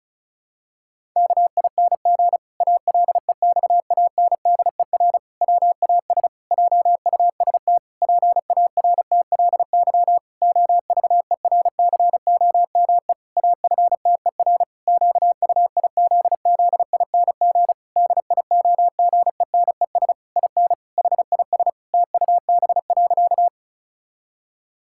Morse code, heard as a keyed tone, about 36 wpm, 700 Hz. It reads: KING ALEXANDER WAS JUST PARTLY OVERCOME AFTER QUIZZING DIOGENES IN HIS TUB.
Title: Pangram Challenge v2 - 35wpm